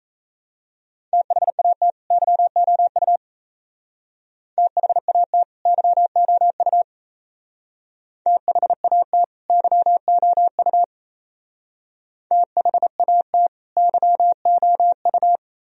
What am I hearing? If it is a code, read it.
Morse code, heard as a keyed tone, about 36 wpm, 700 Hz. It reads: THAT YOU THAT YOU THAT YOU THAT YOU